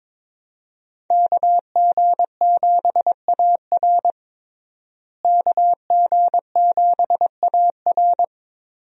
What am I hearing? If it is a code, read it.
Morse code, heard as a keyed tone, about 22 wpm, 700 Hz. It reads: KG7AR KG7AR
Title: USA Call Signs - 5 Characters - Repeated 3x - 22wpm